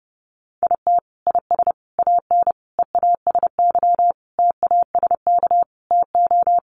Morse code, heard as a keyed tone, about 30 wpm, 700 Hz. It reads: IT IS AN EASY TASK TO
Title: Sentences from Top 1500 Words 30wpm